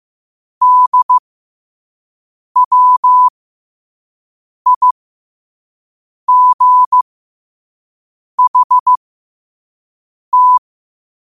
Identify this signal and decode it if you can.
Morse code, heard as a keyed tone, about 15 wpm, spaced as for 5 wpm, 1000 Hz. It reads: DWIGHT
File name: qotdfiles / 5wpm